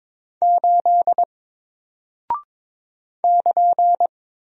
Morse code, heard as a keyed tone, about 22 wpm, 700 Hz. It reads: 8 <KN>
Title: Single Character (Letter:Number:Punctuation:Prosign) Alt 22wpm